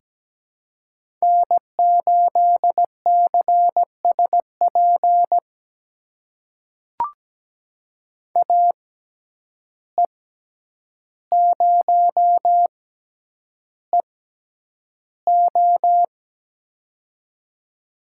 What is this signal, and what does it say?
Morse code, heard as a keyed tone, about 17 wpm, spaced as for 9 wpm, 700 Hz. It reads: N8CSP AE0EO
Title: USA Call Signs - 5 Characters - 6x Character Spacing - 17wpm